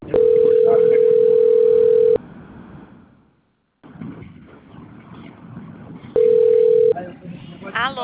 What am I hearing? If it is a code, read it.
Ringback tone (North America), 440 + 480 Hz, with speech.
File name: out- - - - - .63833